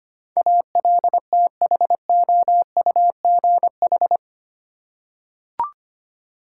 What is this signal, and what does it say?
Morse code, heard as a keyed tone, about 25 wpm, 700 Hz. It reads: ALTHOUGH
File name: 8 Letter Words - First 3 Letters Given - 25wpm